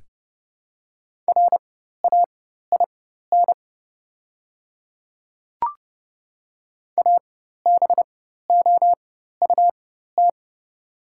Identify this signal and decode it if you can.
Morse code, heard as a keyed tone, about 30 wpm, spaced as for 14 wpm, 700 Hz. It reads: RAIN ABOUT
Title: 4x Character Spacing - Top 300 Words - 30wpm